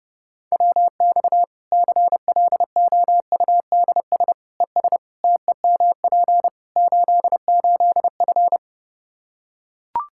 Morse code, heard as a keyed tone, about 32 wpm, 700 Hz. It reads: WX CLOUDS ES TEMP 88F
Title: K1USN SST QSO Practice 30wpm